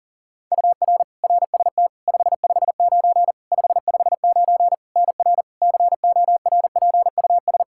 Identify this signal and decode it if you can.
Morse code, heard as a keyed tone, about 37 wpm, 700 Hz. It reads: UR RST 559 559 NR CORPUS
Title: K1USN SST QSO Practice 40wpm